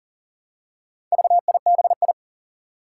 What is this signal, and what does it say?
Morse code, heard as a keyed tone, about 40 wpm, 700 Hz. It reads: VIBI
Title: Four Challenging Letter-Number Combinations 40wpm